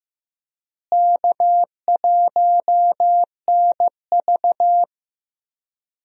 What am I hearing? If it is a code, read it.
Morse code, heard as a keyed tone, about 15 wpm, 700 Hz. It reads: K1NV
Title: USA Call Signs - 4 Characters - Repeated 3x - 15wpm